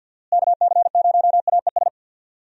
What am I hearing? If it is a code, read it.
Morse code, heard as a keyed tone, about 51 wpm, 700 Hz. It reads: KC0AES
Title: USA Call Signs - 6 Characters - Repeated 3x - 50wpm